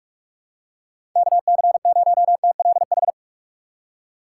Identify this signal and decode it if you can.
Morse code, heard as a keyed tone, about 44 wpm, 700 Hz. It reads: K C 0 T L H
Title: USA Call Signs - 6 Characters - 6x Character Spacing - 45wpm